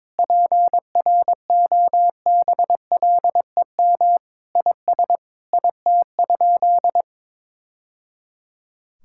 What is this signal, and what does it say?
Morse code, heard as a keyed tone, about 22 wpm, 700 Hz. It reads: PROBLEM IS IT?
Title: Sentences from Top 1000 Words - Review Part 6 - 22wpm